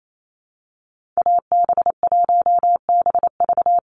Morse code, heard as a keyed tone, about 28 wpm, 700 Hz. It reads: AB1BV
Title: USA Call Signs 28wpm